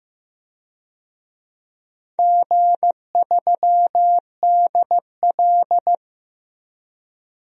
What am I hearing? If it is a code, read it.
Morse code, heard as a keyed tone, about 16 wpm, 700 Hz. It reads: G3DL
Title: DXCC-100 Call Signs - 4 Characters - Repeated 3x - 15wpm